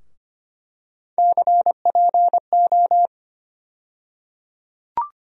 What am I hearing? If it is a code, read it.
Morse code, heard as a keyed tone, about 25 wpm, 700 Hz. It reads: CPO
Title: USA Call Sign Postfixes - 3 Letters 25wpm